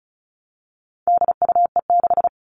Morse code, heard as a keyed tone, about 34 wpm, 700 Hz. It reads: DUE6
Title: All Cut Numbers - 4-Digit 35wpm